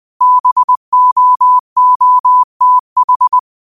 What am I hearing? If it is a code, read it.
Morse code, heard as a keyed tone, about 20 wpm, 1000 Hz. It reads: BOOTH
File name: 20wpm